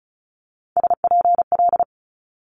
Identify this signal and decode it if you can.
Morse code, heard as a keyed tone, about 36 wpm, 700 Hz. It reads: SPL
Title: Multi-letter Phonemes Quick Fire 35wpm